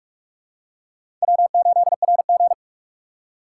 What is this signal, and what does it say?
Morse code, heard as a keyed tone, about 46 wpm, 700 Hz. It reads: W8RG
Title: USA Call Signs - 4 Characters - 2x Character Spacing - 45wpm